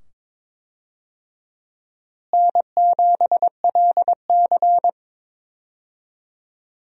Morse code, heard as a keyed tone, about 22 wpm, 700 Hz. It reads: N7LC
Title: USA Call Signs - 4 Characters - Repeated 2x - 22wpm